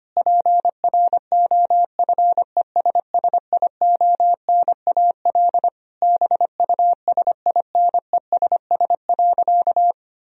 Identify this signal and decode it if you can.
Morse code, heard as a keyed tone, about 25 wpm, 700 Hz. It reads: PROFESSIONAL BUSINESS.
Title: Sentences from Top 1000 Words - Review Part 4 - 25wpm